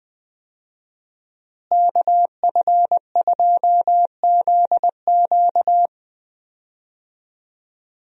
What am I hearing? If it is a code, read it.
Morse code, heard as a keyed tone, about 21 wpm, 700 Hz. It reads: KF2ZQ
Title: USA Call Signs - 5 Characters - Repeated 3x - 20wpm